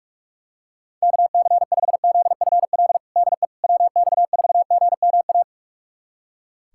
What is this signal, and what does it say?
Morse code, heard as a keyed tone, about 46 wpm, 700 Hz. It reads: KC5ZFL DE WX4GMA
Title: USA Callsign DE Callsign - 6 Characters - 45wpm